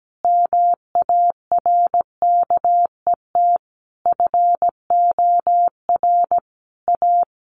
Morse code, heard as a keyed tone, about 17 wpm, 700 Hz. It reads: MARKET FOR A
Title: Sentences from Top 600 Words 17wpm